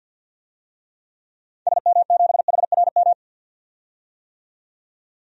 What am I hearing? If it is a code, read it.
Morse code, heard as a keyed tone, about 52 wpm, 700 Hz. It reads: SM7HRM
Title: DXCC-100 Call Signs - 6 Characters - 50wpm